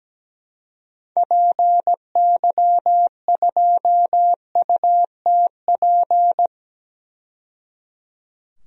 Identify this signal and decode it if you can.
Morse code, heard as a keyed tone, about 17 wpm, 700 Hz. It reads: PY2UTP
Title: DXCC-100 Call Signs - 6 Characters - 17wpm